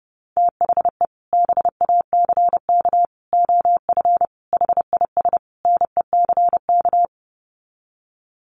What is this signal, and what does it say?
Morse code, heard as a keyed tone, about 30 wpm, 700 Hz. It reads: THE BACK OF HIS NECK
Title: Sets of 5 Words Encore 30wpm